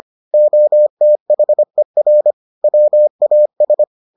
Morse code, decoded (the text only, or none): OTHER WAS